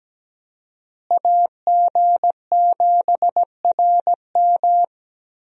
Morse code, heard as a keyed tone, about 17 wpm, 700 Hz. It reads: AG7RM
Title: USA Call Signs - 5 Characters - Repeated 3x - 17wpm